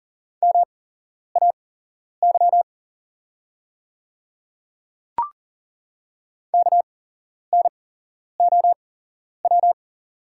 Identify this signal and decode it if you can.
Morse code, heard as a keyed tone, about 40 wpm, spaced as for 11 wpm, 690 Hz. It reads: MAY KNOW
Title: 8x Character Spacing - Top 100 Words - 40wpm